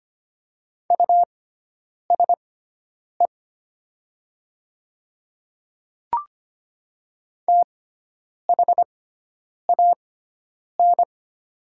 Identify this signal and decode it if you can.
Morse code, heard as a keyed tone, about 25 wpm, spaced as for 9 wpm, 700 Hz. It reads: USE THAN
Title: 6x Character Spacing - Top 100 Words - 25wpm